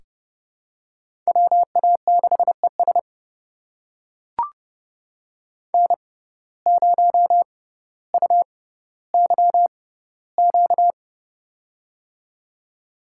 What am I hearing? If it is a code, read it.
Morse code, heard as a keyed tone, about 30 wpm, spaced as for 15 wpm, 700 Hz. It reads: WA6ES N0UYQ
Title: USA Call Signs - 5 Characters - 6x Character Spacing - 30wpm